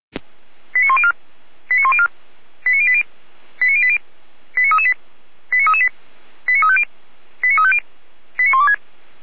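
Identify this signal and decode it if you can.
Five-tone selective call (ZVEI), heard as quick runs of five tones, tones of about 70 ms. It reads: 89104 89104 89080 89080 89338 89338 89350 89350 89126
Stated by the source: ZVEI five-tone selective call received by SDR.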